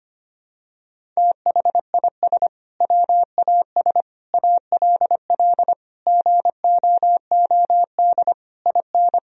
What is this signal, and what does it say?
Morse code, heard as a keyed tone, about 25 wpm, 700 Hz. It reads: THIS WAS ALL GOOD IN
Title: Sentences from Top 1000 Words - Review Part 3 - 25wpm